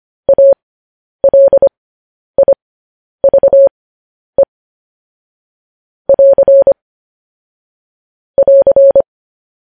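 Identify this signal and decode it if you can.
Morse code, heard as a keyed tone, about 25 wpm, spaced as for 10 wpm, 550 Hz. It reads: ALIVE <AR> <AR>
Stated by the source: Morse code news headlines transmission.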